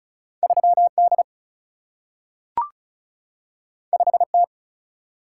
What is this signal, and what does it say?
Morse code, heard as a keyed tone, about 36 wpm, 700 Hz. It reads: 3 D 5 T
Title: Single-Letter Single-Number 35wpm